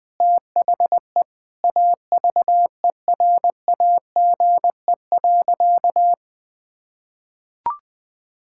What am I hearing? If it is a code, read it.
Morse code, heard as a keyed tone, about 21 wpm, 700 Hz. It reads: THE AVERAGE.
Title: Sentences from Top 1000 Words - Review Part 6 - 20wpm